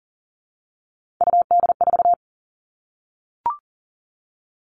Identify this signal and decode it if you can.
Morse code, heard as a keyed tone, about 40 wpm, 700 Hz. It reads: UD4
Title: All Cut Numbers - 3-Digit 40wpm